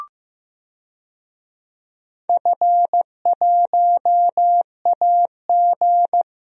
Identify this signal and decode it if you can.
Morse code, heard as a keyed tone, about 15 wpm, 700 Hz. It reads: F1AG
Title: DXCC-100 Call Signs - 4 Characters - 15wpm